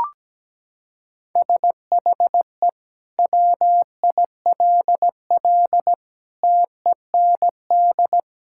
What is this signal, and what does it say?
Morse code, heard as a keyed tone, about 17 wpm, 700 Hz. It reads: SHE WILL TEND
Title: Sentences from Top 1000 Words - Review Part 5 - 17wpm